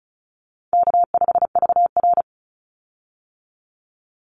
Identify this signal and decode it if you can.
Morse code, heard as a keyed tone, about 34 wpm, 700 Hz. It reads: K5VR
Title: Call Signs Repeated 2x - 35wpm